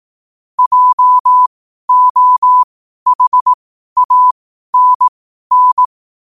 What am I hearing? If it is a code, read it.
Morse code, heard as a keyed tone, about 18 wpm, spaced as for 12 wpm, 1000 Hz. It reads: JOHANN
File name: qotdfiles / 12wpm